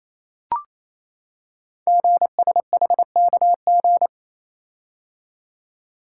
Morse code, heard as a keyed tone, about 29 wpm, 700 Hz. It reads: GSHKG
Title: Code Groups - 5 Letters - 28wpm